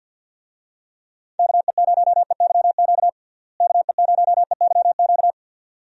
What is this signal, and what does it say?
Morse code, heard as a keyed tone, about 53 wpm, 700 Hz. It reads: KE0EYQ KE0EYQ
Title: USA Call Signs - 6 Characters - Repeated 2x - 50wpm